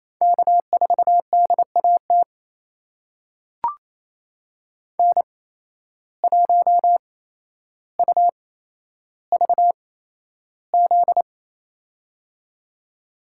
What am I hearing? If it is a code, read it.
Morse code, heard as a keyed tone, about 28 wpm, spaced as for 12 wpm, 700 Hz. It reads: K4DAT N1UVZ
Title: USA Call Signs - 5 Characters - 8x Character Spacing - 28wpm